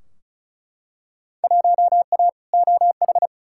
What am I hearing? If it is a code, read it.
Morse code, heard as a keyed tone, about 36 wpm, 700 Hz. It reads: 1A OH
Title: Field Day Exchanges - 35wpm